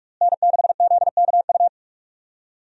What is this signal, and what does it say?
Morse code, heard as a keyed tone, about 42 wpm, 700 Hz. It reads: N6ZKU
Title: USA Call Signs - 5 Characters - Repeated 2x - 45wpm